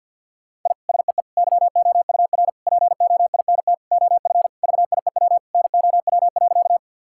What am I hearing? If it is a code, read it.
Morse code, heard as a keyed tone, about 50 wpm, 700 Hz. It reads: I SEE YOUR POINT OF VIEW NOW.
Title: Sentences from Top 1000 Words - Review Part 6 - 50wpm